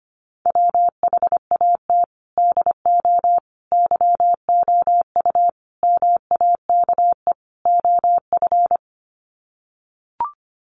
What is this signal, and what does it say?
Morse code, heard as a keyed tone, about 25 wpm, 700 Hz. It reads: WHAT DO YOU MAKE OF
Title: Sets of 5 Words 25wpm